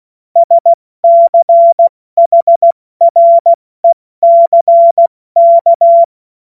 Morse code, heard as a keyed tone, about 16 wpm, 680 Hz. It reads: SCHRECK
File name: Morsekurs / Witze / Witz 27. Juli 2021, 70 F80